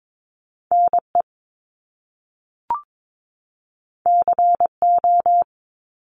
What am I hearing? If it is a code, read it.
Morse code, heard as a keyed tone, about 22 wpm, 700 Hz. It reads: NE CO
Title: Top Two Letter Combinations in Words 22wpm